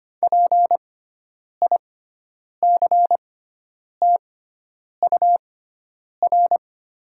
Morse code, heard as a keyed tone, about 25 wpm, spaced as for 9 wpm, 700 Hz. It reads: PICTUR
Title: 6x Character Spacing - Top 300 Words - 25wpm